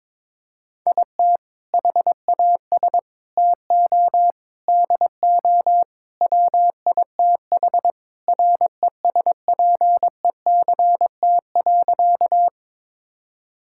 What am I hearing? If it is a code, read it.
Morse code, heard as a keyed tone, about 22 wpm, 700 Hz. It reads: IT HAS TO DO WITH RESPECT.
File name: practice / 22wpm / Sentences with 4-Word Phrases - 22wpm